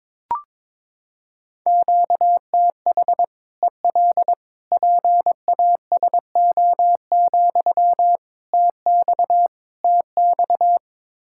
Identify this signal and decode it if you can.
Morse code, heard as a keyed tone, about 22 wpm, 700 Hz. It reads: QTH EL PASO, TX TX
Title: Practice QSO - Exchange 1 - Beginning - 22wpm